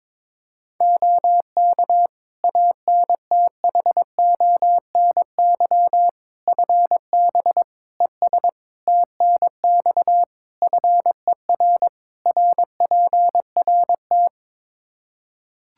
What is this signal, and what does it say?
Morse code, heard as a keyed tone, about 22 wpm, 700 Hz. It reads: OK ANTHONY FB ES TNX FER RPRT
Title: Practice QSO - Exchange 2 - Info - 22wpm